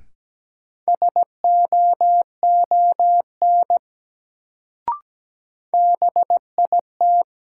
Morse code, heard as a keyed tone, about 17 wpm, 700 Hz. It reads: SOON BIT
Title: Sets of 3 Words - New Word Drill - 17wpm